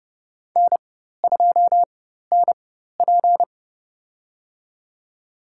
Morse code, heard as a keyed tone, about 30 wpm, spaced as for 14 wpm, 700 Hz. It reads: N2NP
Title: USA Call Signs - 4x Character Spacing - 30wpm